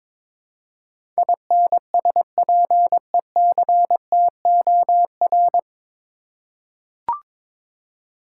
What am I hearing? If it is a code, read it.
Morse code, heard as a keyed tone, about 22 wpm, 700 Hz. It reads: INSPECTOR
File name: 9 Letter Words - First 3 Letters Given - 22wpm